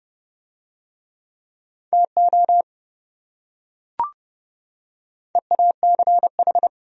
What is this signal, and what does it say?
Morse code, heard as a keyed tone, about 28 wpm, 700 Hz. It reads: TO EACH